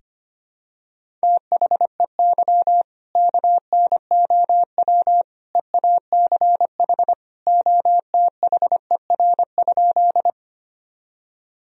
Morse code, heard as a keyed tone, about 25 wpm, 700 Hz. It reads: THEY KNOW EACH OTHER?
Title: Sentences from Top 1000 Words - Review Part 1 - 25wpm